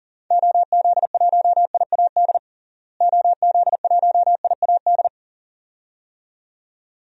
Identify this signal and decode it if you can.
Morse code, heard as a keyed tone, about 40 wpm, 700 Hz. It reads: OZ1IAD OZ1IAD
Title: DXCC-100 Call Signs - 6 Characters - Repeated 2x - 40wpm